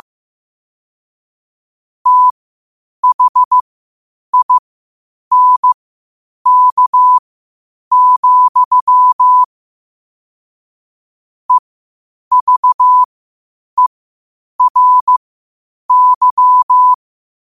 Morse code, heard as a keyed tone, about 15 wpm, spaced as for 8 wpm, 1000 Hz. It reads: THINK, EVERY
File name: qotdfiles / 8wpm